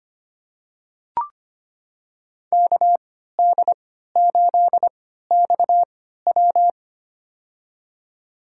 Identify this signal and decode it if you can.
Morse code, heard as a keyed tone, about 25 wpm, spaced as for 14 wpm, 700 Hz. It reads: KD8XW
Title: USA Call Signs - 5 Characters - 3x Character Spacing - 25wpm